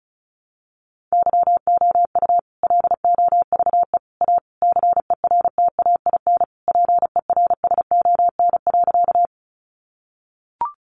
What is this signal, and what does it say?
Morse code, heard as a keyed tone, about 34 wpm, 700 Hz. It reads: YOU LOVE A CERTAIN PERSON.
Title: Sentences from Top 500 Words 35wpm